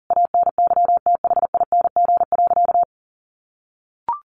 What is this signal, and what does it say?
Morse code, heard as a keyed tone, about 40 wpm, 700 Hz. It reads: ANYTHING.
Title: A Sound Of Thunder 40wpm